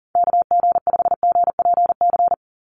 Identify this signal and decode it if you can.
Morse code, heard as a keyed tone, about 40 wpm, 700 Hz. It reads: KG5GPC
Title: USA Call Signs 40wpm